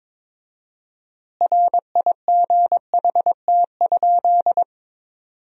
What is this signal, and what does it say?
Morse code, heard as a keyed tone, about 22 wpm, 700 Hz. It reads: RIGHT?
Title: 4x Word-Spacing - Sentences from Top 200 Words - 22wpm